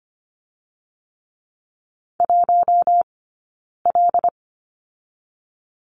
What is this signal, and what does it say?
Morse code, heard as a keyed tone, about 25 wpm, 700 Hz. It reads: 1 L
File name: course / 25:10wpm / 092 Lesson 16d - Letter c - Two Characters 25wpm